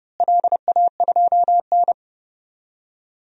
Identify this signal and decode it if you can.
Morse code, heard as a keyed tone, about 31 wpm, 700 Hz. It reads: LA2N
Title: DXCC-100 Call Signs - 4 Characters - Repeated 2x - 30wpm